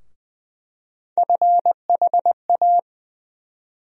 Morse code, heard as a keyed tone, about 20 wpm, 700 Hz. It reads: FHA
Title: USA Call Sign Postfixes - 3 Letters 20wpm